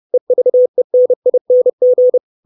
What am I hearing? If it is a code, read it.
Morse code, heard as a keyed tone, about 30 wpm, 500 Hz. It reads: EVENING
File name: 632 Words 30WPM